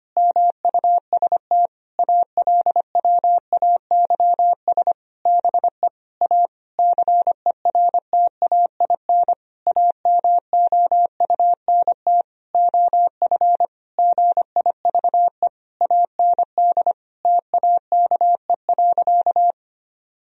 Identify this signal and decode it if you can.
Morse code, heard as a keyed tone, about 25 wpm, 700 Hz. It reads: MUST ALWAYS BE A CERTAIN AMOUNT OF GIVE AND TAKE.
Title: Sentences from Top 1000 Words - Review Part 4 - 25wpm